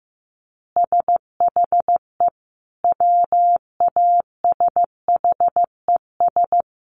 Morse code, heard as a keyed tone, about 15 wpm, 700 Hz. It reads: SHE WASHES
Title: Easy Phrases 15wpm